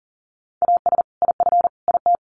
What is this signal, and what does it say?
Morse code, heard as a keyed tone, about 40 wpm, 700 Hz. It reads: AS IF IT
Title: Sets of 5 Words 40wpm